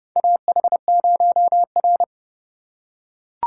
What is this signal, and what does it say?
Morse code, heard as a keyed tone, about 30 wpm, 690 Hz. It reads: AH0R
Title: USA Call Signs - 4 Characters - 2x Character Spacing - 30wpm